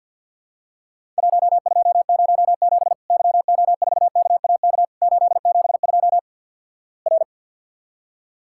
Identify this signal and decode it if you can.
Morse code, heard as a keyed tone, about 50 wpm, 700 Hz. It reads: 1207 YO4KAK 872 R
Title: Worked All Europe DX Contest - QTC Exchange with Repeats - 50wpm